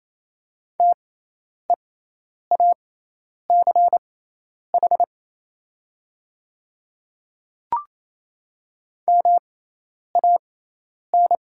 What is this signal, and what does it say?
Morse code, heard as a keyed tone, about 28 wpm, spaced as for 10 wpm, 700 Hz. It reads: TEACH MAN
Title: 6x Character Spacing - Top 500 Words - 28wpm